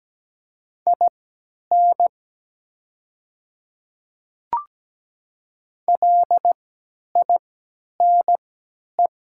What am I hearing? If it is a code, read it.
Morse code, heard as a keyed tone, about 17 wpm, spaced as for 10 wpm, 700 Hz. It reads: IN LINE
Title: 3x Character Spacing - Top 200 Words - 17wpm